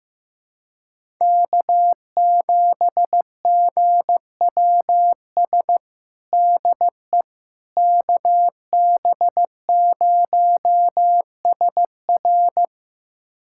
Morse code, heard as a keyed tone, about 15 wpm, 700 Hz. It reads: K7GWS DE KB0SR
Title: USA Callsign DE Callsign - 5 Characters - 15wpm